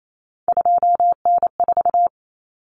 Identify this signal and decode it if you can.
Morse code, heard as a keyed tone, about 28 wpm, 700 Hz. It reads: 2N4
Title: Common Cut Numbers - 3-Digit 28wpm